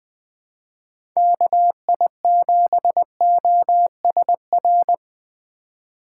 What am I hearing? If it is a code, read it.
Morse code, heard as a keyed tone, about 21 wpm, 700 Hz. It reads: KI7OSR
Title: USA Call Signs - 6 Characters - 4x Character Spacing - 20wpm